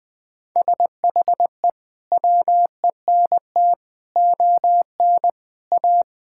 Morse code, heard as a keyed tone, about 20 wpm, 700 Hz. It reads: SHE WENT ON A
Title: Sentences with 2-Word Phrases - Spoken Beginning - 20wpm